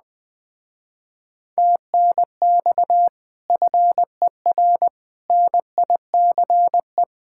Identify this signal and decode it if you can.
Morse code, heard as a keyed tone, about 20 wpm, 700 Hz. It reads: TNX FER NICE